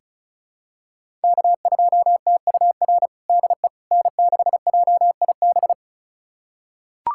Morse code, heard as a keyed tone, about 36 wpm, 700 Hz. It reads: K2TUR DE N6JIB
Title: USA Callsign DE Callsign - 5 Characters - 35wpm